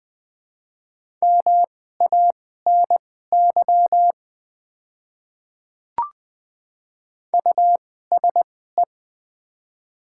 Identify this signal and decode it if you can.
Morse code, heard as a keyed tone, about 20 wpm, spaced as for 14 wpm, 700 Hz. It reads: MANY USE
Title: 2x Character Spacing - Top 100 Words - 20wpm